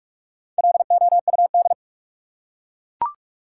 Morse code, heard as a keyed tone, about 44 wpm, 700 Hz. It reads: LOUD
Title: Four Letter Words - 45wpm